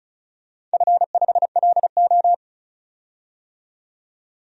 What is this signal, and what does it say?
Morse code, heard as a keyed tone, about 34 wpm, 700 Hz. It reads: F5LO
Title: DXCC-100 Call Signs - 4 Characters - 35wpm